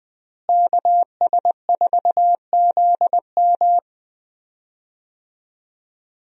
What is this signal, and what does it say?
Morse code, heard as a keyed tone, about 20 wpm, 700 Hz. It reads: KS4ZM
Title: USA Call Signs - 5 Characters - Repeated 2x - 20wpm